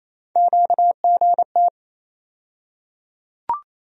Morse code, heard as a keyed tone, about 28 wpm, 700 Hz. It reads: QGT
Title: USA Call Sign Postfixes - 3 Letters 28wpm